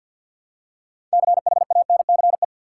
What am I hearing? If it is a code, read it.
Morse code, heard as a keyed tone, about 51 wpm, 700 Hz. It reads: CHANCE
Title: Six Letter Words - 50wpm